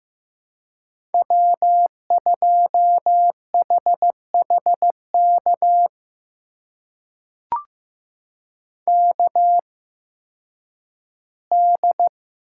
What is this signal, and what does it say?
Morse code, heard as a keyed tone, about 15 wpm, spaced as for 10 wpm, 700 Hz. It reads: W2HHK KD
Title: USA Call Signs - 5 Characters - 8x Character Spacing - 15wpm